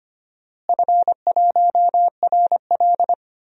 Morse code, heard as a keyed tone, about 25 wpm, 700 Hz. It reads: F1RL
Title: DXCC-100 Call Signs - 4 Characters - Repeated 2x - 25wpm